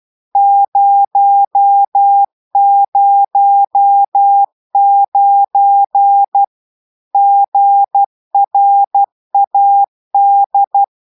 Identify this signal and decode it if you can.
Morse code, heard as a keyed tone, about 12 wpm, 800 Hz. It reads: 009 GRAD